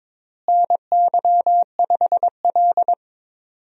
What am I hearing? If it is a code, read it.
Morse code, heard as a keyed tone, about 21 wpm, 700 Hz. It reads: NY5L
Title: USA Call Signs - 4 Characters - Repeated 2x - 22wpm